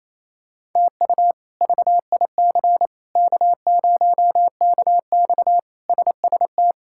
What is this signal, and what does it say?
Morse code, heard as a keyed tone, about 28 wpm, 700 Hz. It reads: TU VIC K0KX SST
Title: K1USN SST QSO Practice - 28wpm